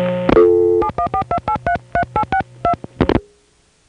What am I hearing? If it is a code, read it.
DTMF, heard as telephone touch tones, tones of about 90 ms. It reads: 7142833862